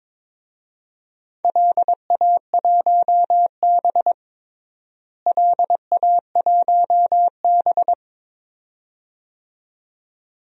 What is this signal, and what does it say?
Morse code, heard as a keyed tone, about 21 wpm, 700 Hz. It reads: LA1B LA1B
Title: DXCC-100 Call Signs - 4 Characters - Repeated 2x - 22wpm